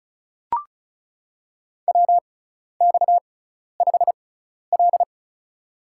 Morse code, heard as a keyed tone, about 36 wpm, spaced as for 12 wpm, 700 Hz. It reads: WX5L
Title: USA Call Signs - 4 Characters - 6x Character Spacing - 35wpm